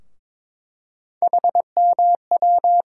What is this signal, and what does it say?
Morse code, heard as a keyed tone, about 21 wpm, 700 Hz. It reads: HMW
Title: USA Call Sign Postfixes - 3 Letters 22wpm